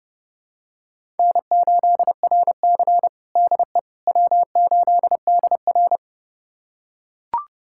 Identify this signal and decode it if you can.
Morse code, heard as a keyed tone, about 30 wpm, 700 Hz. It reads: N8RC DE W8DR
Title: USA Callsign DE Callsign - 4 Characters - 30wpm